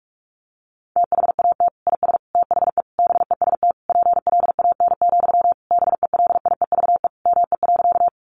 Morse code, heard as a keyed tone, about 44 wpm, 700 Hz. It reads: THAT IS THE BEST PLAN, BELIEVE ME.
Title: Sentences from Top 1000 Words 45wpm